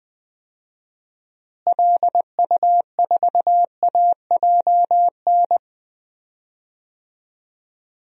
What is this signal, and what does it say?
Morse code, heard as a keyed tone, about 20 wpm, 700 Hz. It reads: LU4AJN
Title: DXCC-100 Call Signs - 6 Characters - Repeated 2x - 20wpm